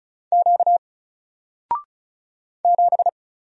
Morse code, heard as a keyed tone, about 36 wpm, 700 Hz. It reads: Q 7
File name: Single-Letter-Number-35wpm